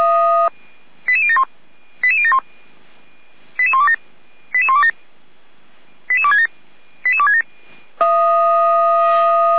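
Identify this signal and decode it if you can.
Five-tone selective call (ZVEI), heard as quick runs of five tones, tones of about 70 ms. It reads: 88071 88071 88127 88127 88267 88267
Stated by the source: ZVEI five-tone call received by SDR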